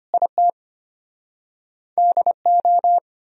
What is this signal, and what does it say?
Morse code, heard as a keyed tone, about 33 wpm, 700 Hz. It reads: IT DO